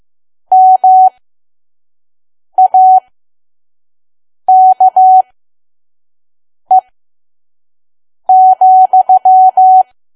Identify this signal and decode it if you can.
Morse code, heard as a keyed tone, about 15 wpm, spaced as for 5 wpm, 750 Hz. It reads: MAKE,
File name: Archive / 5 WPM files / PM